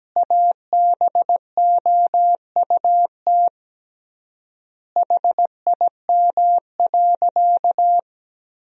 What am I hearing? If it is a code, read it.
Morse code, heard as a keyed tone, about 17 wpm, 700 Hz. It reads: ABOUT HIM.
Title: 3x Word-Spacing - Sentences from Top 200 Words - 17wpm